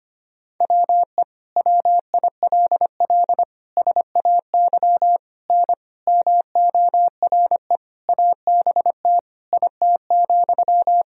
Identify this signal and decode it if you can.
Morse code, heard as a keyed tone, about 25 wpm, 700 Hz. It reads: WE WILL SAY N MORE ABT IT,
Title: Sentences from Top 200 Words and Abbreviations 25wpm